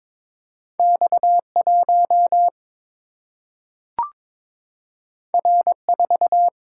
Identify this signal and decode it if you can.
Morse code, heard as a keyed tone, about 22 wpm, 700 Hz. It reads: X 1 R 4
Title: Single-Letter Single-Number 22wpm